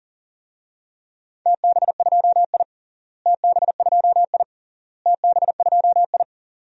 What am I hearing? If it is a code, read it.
Morse code, heard as a keyed tone, about 40 wpm, 700 Hz. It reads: TB2I TB2I TB2I
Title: DXCC-100 Call Signs - 4 Characters - Repeated 3x - 40wpm